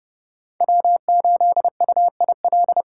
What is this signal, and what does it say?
Morse code, heard as a keyed tone, about 30 wpm, 700 Hz. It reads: W8UIL
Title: K1USN SST QSO Practice - 30wpm